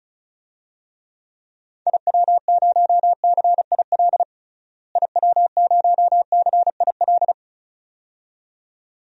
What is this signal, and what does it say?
Morse code, heard as a keyed tone, about 36 wpm, 700 Hz. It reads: IW0CIL IW0CIL
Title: DXCC-100 Call Signs - 6 Characters - Repeated 2x - 35wpm